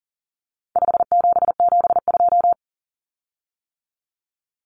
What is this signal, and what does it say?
Morse code, heard as a keyed tone, about 40 wpm, 700 Hz. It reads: 5772
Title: Common Cut Numbers - 4-Digit 40wpm